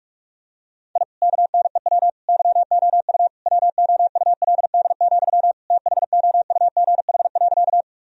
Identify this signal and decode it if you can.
Morse code, heard as a keyed tone, about 42 wpm, 700 Hz. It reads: I KNEW YOU WOULD, THOUGH.
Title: Sentences from Top 1000 Words - Review Part 2 - 45wpm